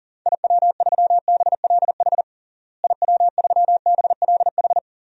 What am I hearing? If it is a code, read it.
Morse code, heard as a keyed tone, about 40 wpm, 700 Hz. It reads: IW3BLH IW3BLH
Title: DXCC-100 Call Signs - 6 Characters - Repeated 3x - 40wpm